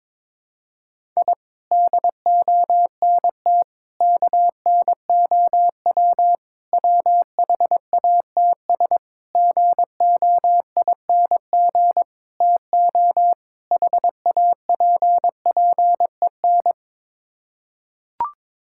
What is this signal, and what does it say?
Morse code, heard as a keyed tone, about 22 wpm, 700 Hz. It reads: I DONT KNOW WHATS GOING TO HAPPEN
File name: practice / 22wpm / Sets of 7 Words - 22wpm